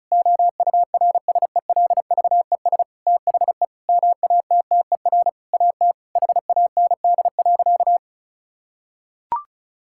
Morse code, heard as a keyed tone, about 36 wpm, 700 Hz. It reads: OURSELVES THE MATTER AT HAND.
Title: Sentences from Top 1600 Words 35wpm